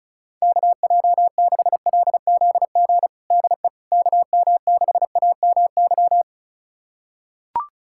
Morse code, heard as a keyed tone, about 34 wpm, 700 Hz. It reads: KJ6LZG DE KM6AMY
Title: USA Callsign DE Callsign - 6 Characters - 35wpm